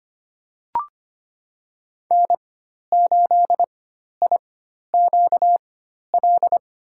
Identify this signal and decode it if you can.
Morse code, heard as a keyed tone, about 25 wpm, spaced as for 12 wpm, 700 Hz. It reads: N8IQL
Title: USA Call Signs - 4x Character Spacing - 25wpm